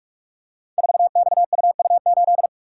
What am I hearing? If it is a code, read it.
Morse code, heard as a keyed tone, about 44 wpm, 700 Hz. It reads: 4XUU8
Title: Code Groups - 5 Letter:Numbers - 45wpm